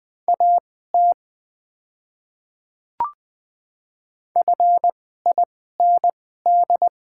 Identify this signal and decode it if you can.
Morse code, heard as a keyed tone, about 20 wpm, spaced as for 15 wpm, 700 Hz. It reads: AT FIND
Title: 2x Character Spacing - Top 100 Words - 20wpm